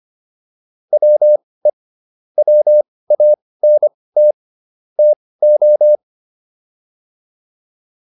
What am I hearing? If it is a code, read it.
Morse code, heard as a keyed tone, about 25 wpm, spaced as for 18 wpm, 600 Hz. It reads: WE WANT TO
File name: WD204 18